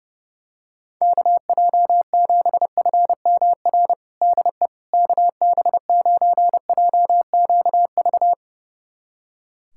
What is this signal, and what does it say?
Morse code, heard as a keyed tone, about 32 wpm, 700 Hz. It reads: KJ7FMR DE KB9JQV
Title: USA Callsign DE Callsign 30wpm